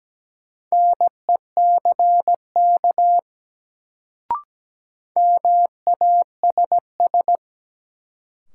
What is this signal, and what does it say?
Morse code, heard as a keyed tone, about 17 wpm, 700 Hz. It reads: NECK MASS
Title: Sets of 5 Words Encore - New Word Drill - 17wpm